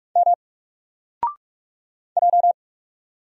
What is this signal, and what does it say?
Morse code, heard as a keyed tone, about 44 wpm, 700 Hz. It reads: M J
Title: Single Letter-Number 45wpm